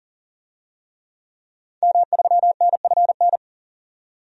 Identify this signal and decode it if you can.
Morse code, heard as a keyed tone, about 40 wpm, 700 Hz. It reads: M3NFN
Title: DXCC-100 Call Signs - 5 Characters - 40wpm